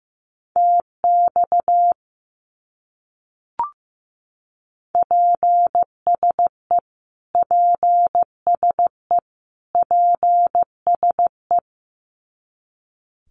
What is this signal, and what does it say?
Morse code, heard as a keyed tone, about 15 wpm, 700 Hz. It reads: TX PSE PSE PSE
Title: Abbreviations Repeated 3x - 15wpm